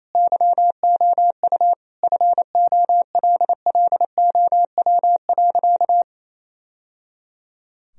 Morse code, heard as a keyed tone, about 28 wpm, 700 Hz. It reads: YOU FOLLOW.
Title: Sentences from Top 1000 Words - Review Part 3 - 28wpm